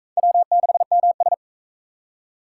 Morse code, heard as a keyed tone, about 38 wpm, 700 Hz. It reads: W6MS